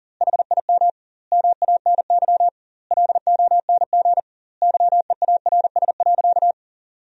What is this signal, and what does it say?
Morse code, heard as a keyed tone, about 40 wpm, 700 Hz. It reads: HIM MANY LONG YEARS.
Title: 2x Word-Spacing - Sentences from Top 200 Words - 40wpm